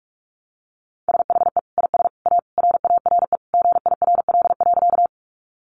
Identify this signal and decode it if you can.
Morse code, heard as a keyed tone, about 44 wpm, 700 Hz. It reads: SHE IS A RARE GIRL.
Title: Sentences from Top 1100 Words 45wpm